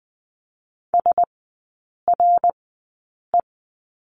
Morse code, heard as a keyed tone, about 20 wpm, spaced as for 8 wpm, 700 Hz. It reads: SRE
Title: 053 Lesson 10e - Letter r - Three Characters 20wpm